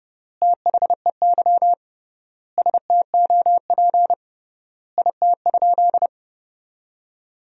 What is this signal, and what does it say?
Morse code, heard as a keyed tone, about 30 wpm, 700 Hz. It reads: THEY STOP IT?
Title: 3x Word-Spacing - Sentences from Top 300 Words - 30wpm